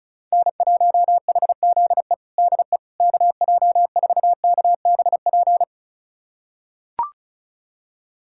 Morse code, heard as a keyed tone, about 34 wpm, 700 Hz. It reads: N1HZE DE KJ4KBP
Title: USA Callsign DE Callsign 35wpm